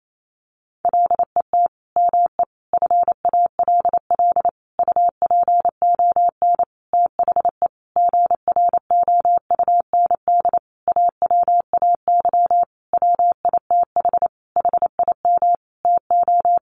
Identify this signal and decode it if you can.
Morse code, heard as a keyed tone, about 28 wpm, 700 Hz. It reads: LET ME FALL UPON THE GROUND AWAY WITH HIM TO
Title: Sentences from Top 1100 Words 28wpm